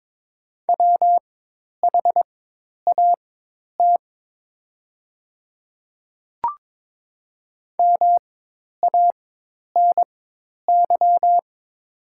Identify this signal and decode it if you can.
Morse code, heard as a keyed tone, about 22 wpm, spaced as for 10 wpm, 700 Hz. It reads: WHAT MANY
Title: 4x Character Spacing - Top 100 Words - 22wpm